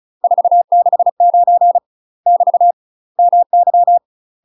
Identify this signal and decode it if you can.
Morse code, heard as a keyed tone, about 34 wpm, 700 Hz. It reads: 469 <BT> MY
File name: UK MASTER QSOs TRAINING / 35WPM / UK MASTER QSOs 0008